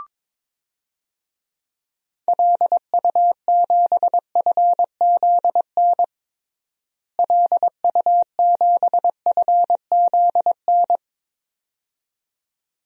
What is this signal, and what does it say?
Morse code, heard as a keyed tone, about 22 wpm, 700 Hz. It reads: LU7FZN LU7FZN
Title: DXCC-100 Call Signs - 6 Characters - Repeated 2x - 22wpm